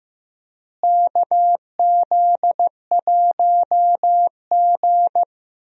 Morse code, heard as a keyed tone, about 15 wpm, 700 Hz. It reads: KZ1G
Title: USA Call Signs - 4 Characters - Repeated 3x - 15wpm